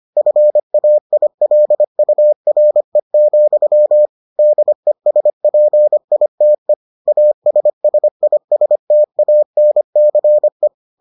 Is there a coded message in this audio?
Yes: Morse code from 0.2 to 10.7 s, heard as a keyed tone, about 25 wpm, 600 Hz. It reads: FAILURE, DESPITE ASSISTANCE